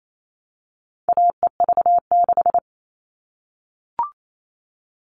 Morse code, heard as a keyed tone, about 28 wpm, 700 Hz. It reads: AEV6
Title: All Cut Numbers - 4-Digit 28wpm